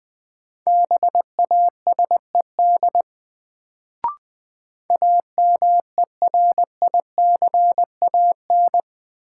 Morse code, heard as a keyed tone, about 21 wpm, 700 Hz. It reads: BASED AMERICAN
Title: Sets of 2 Words - New Word Drill - 20wpm